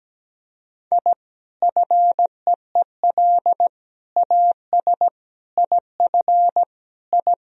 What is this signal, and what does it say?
Morse code, heard as a keyed tone, about 17 wpm, 700 Hz. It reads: I FEEL AS IF I
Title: Sentences from Top 1000 Words - Review Part 3 - 17wpm